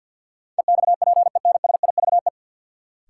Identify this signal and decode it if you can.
Morse code, heard as a keyed tone, about 53 wpm, 700 Hz. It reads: EXPENSIVE
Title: Sentences from Top 1000 Words - Review Part 6 - New Word Drill - 50wpm